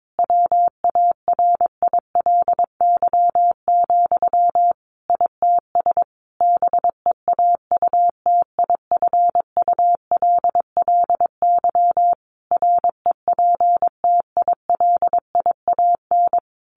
Morse code, heard as a keyed tone, about 22 wpm, 700 Hz. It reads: WARILY, ITS BEAUTIFULLY REPTILIAN
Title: A Sound Of Thunder 22wpm